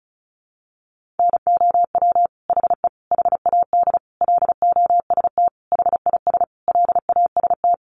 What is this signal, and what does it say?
Morse code, heard as a keyed tone, about 34 wpm, 700 Hz. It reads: NOW HE HAD LOST HIS LAST
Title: Sentences from Top 1400 Words 35wpm